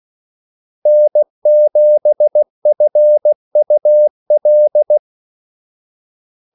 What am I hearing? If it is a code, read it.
Morse code, heard as a keyed tone, about 16 wpm, 600 Hz. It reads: N7FUL
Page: Kaldesignaler 16 wpm